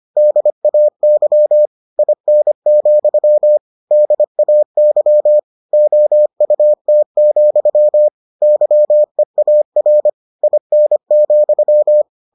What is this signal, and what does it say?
Morse code, heard as a keyed tone, about 25 wpm, 600 Hz. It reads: DAY IN, DAY OUT, YEAR IN,